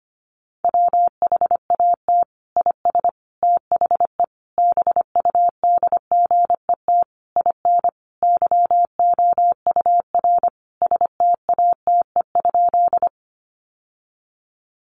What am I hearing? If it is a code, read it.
Morse code, heard as a keyed tone, about 25 wpm, 700 Hz. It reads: WHAT IS THE BUDGET IN YOUR STATE?
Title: Sentences from Top 1200 Words 25wpm